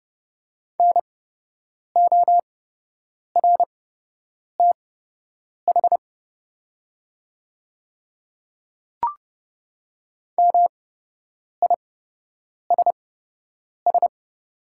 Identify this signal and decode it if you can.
Morse code, heard as a keyed tone, about 31 wpm, spaced as for 8 wpm, 690 Hz. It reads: NORTH MISS
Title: 8x Character Spacing - Top 500 Words - 30wpm